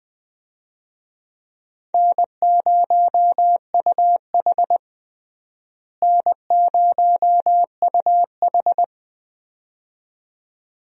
Morse code, heard as a keyed tone, about 20 wpm, 700 Hz. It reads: N0UH N0UH
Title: USA Call Signs - 4 Characters - Repeated 2x - 20wpm